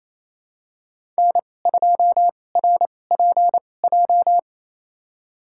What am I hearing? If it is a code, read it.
Morse code, heard as a keyed tone, about 28 wpm, spaced as for 20 wpm, 700 Hz. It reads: N2RPJ
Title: USA Call Signs - 2x Character Spacing - 28wpm